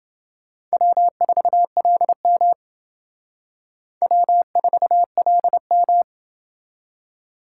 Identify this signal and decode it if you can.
Morse code, heard as a keyed tone, about 28 wpm, 700 Hz. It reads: W4LM W4LM